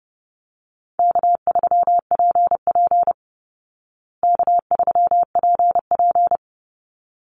Morse code, heard as a keyed tone, about 30 wpm, 700 Hz. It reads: K3PP K3PP
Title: Easy Call Signs Repeated 2x - 30wpm